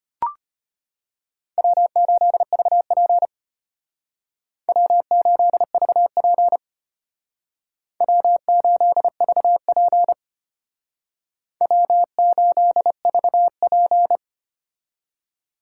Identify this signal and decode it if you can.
Morse code, heard as a keyed tone, about 33 wpm, 700 Hz. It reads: W8VP W8VP W8VP W8VP